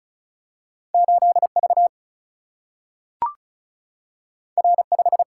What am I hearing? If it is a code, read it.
Morse code, heard as a keyed tone, about 36 wpm, 700 Hz. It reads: 8V R5
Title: Single-Letter Single-Number 35wpm